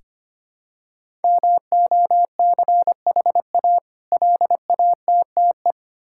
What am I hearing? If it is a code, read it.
Morse code, heard as a keyed tone, about 25 wpm, 700 Hz. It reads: MOCHA LATTE
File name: practice / 25wpm / Difficult Words with Hints - 25wpm